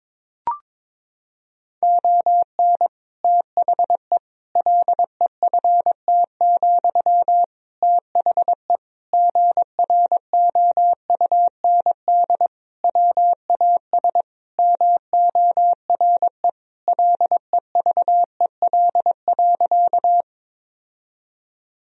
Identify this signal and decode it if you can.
Morse code, heard as a keyed tone, about 22 wpm, 700 Hz. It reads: ON THE LEFT, THE GROUND WAS MORE LEVEL.
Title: Sentences from Top 1000 Words - Review Part 4 - 22wpm